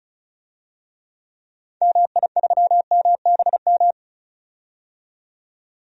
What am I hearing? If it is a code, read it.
Morse code, heard as a keyed tone, about 36 wpm, 700 Hz. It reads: MI3MBM
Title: DXCC-100 Call Signs - 6 Characters - 35wpm